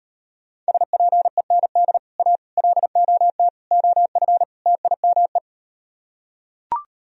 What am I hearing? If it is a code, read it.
Morse code, heard as a keyed tone, about 39 wpm, 700 Hz. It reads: SPEND A LOT OF TIME